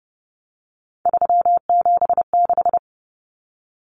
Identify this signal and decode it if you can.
Morse code, heard as a keyed tone, about 28 wpm, 700 Hz. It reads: 376
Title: Common Cut Numbers - 3-Digit 30wpm